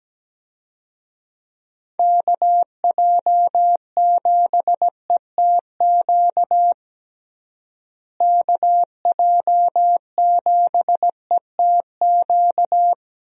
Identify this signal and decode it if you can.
Morse code, heard as a keyed tone, about 17 wpm, 700 Hz. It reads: KJ7ETQ KJ7ETQ
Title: USA Call Signs - 6 Characters - Repeated 2x - 17wpm